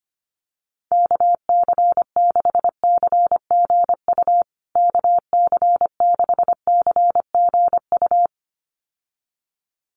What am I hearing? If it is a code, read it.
Morse code, heard as a keyed tone, about 25 wpm, 700 Hz. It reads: KC6CGU KC6CGU
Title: USA Call Signs Repeated 2x 25wpm